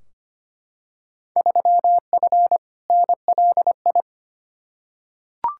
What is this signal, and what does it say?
Morse code, heard as a keyed tone, about 25 wpm, 700 Hz. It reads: 3F NLI
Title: Field Day Exchanges Encore - 25wpm